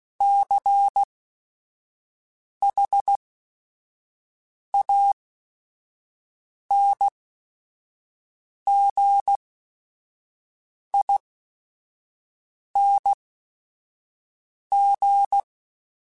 Morse code, heard as a keyed tone, about 16 wpm, spaced as for 5 wpm, 800 Hz. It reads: CHANGING
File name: Special INGs / Special INGs-mp3s / Sngl Ing @5